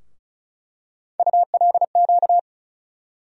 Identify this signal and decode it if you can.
Morse code, heard as a keyed tone, about 36 wpm, 700 Hz. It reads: ULQ
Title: USA Call Sign Postfixes - 3 Letters 35wpm